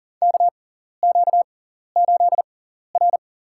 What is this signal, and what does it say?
Morse code, heard as a keyed tone, about 39 wpm, spaced as for 14 wpm, 700 Hz. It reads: KQ8R
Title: USA Call Signs - 6x Character Spacing - 40wpm